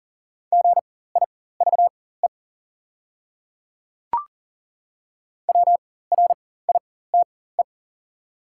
Morse code, heard as a keyed tone, about 40 wpm, spaced as for 19 wpm, 690 Hz. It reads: GIVE WRITE
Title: 4x Character Spacing - Top 200 Words - 40wpm